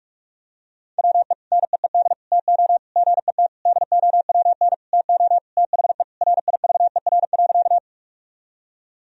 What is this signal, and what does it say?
Morse code, heard as a keyed tone, about 46 wpm, 700 Hz. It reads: WE NEED TO GET DOWN TO THE RIVER.
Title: Sentences from Top 1000 Words - Review Part 1 - 45wpm